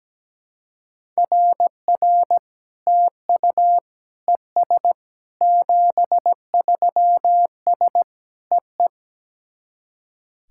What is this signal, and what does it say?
Morse code, heard as a keyed tone, about 17 wpm, 700 Hz. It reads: RR TU ES 73S EE
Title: SOTA QSO Practice 17wpm